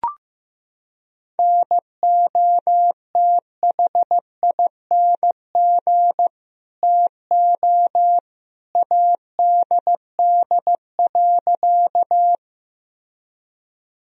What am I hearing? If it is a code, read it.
Morse code, heard as a keyed tone, about 15 wpm, 700 Hz. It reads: NOTHING TO ADD.
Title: Sentences from Top 1000 Words - Review Part 1 - 15wpm